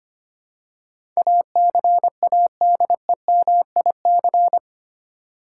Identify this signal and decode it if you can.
Morse code, heard as a keyed tone, about 25 wpm, 700 Hz. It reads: ACADEMIC
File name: Eight Letter Words - 25wpm